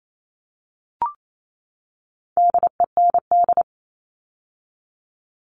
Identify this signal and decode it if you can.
Morse code, heard as a keyed tone, about 28 wpm, 700 Hz. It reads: DEND
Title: All Cut Numbers - 4-Digit 28wpm